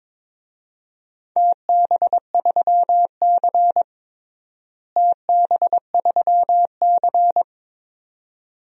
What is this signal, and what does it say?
Morse code, heard as a keyed tone, about 22 wpm, 700 Hz. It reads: TB3C TB3C
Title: DXCC-100 Call Signs - 4 Characters - Repeated 2x - 22wpm